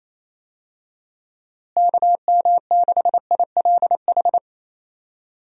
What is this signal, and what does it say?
Morse code, heard as a keyed tone, about 28 wpm, 700 Hz. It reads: KM6ILH
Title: USA Call Signs - 6 Characters - Repeated 2x - 28wpm